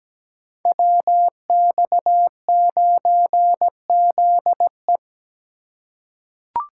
Morse code, heard as a keyed tone, about 17 wpm, 700 Hz. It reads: WX9ZE
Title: Code Groups - 5 Letter:Numbers - 17wpm